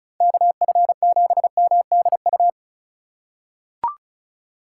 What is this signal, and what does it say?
Morse code, heard as a keyed tone, about 34 wpm, 700 Hz. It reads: KF7MDU
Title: USA Call Signs - 6 Characters - 2x Character Spacing - 35wpm